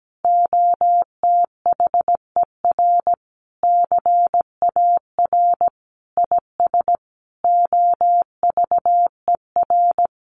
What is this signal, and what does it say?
Morse code, heard as a keyed tone, about 17 wpm, 700 Hz. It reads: OTHER CAR IS OVER
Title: Sentences from Top 1000 Words 17wpm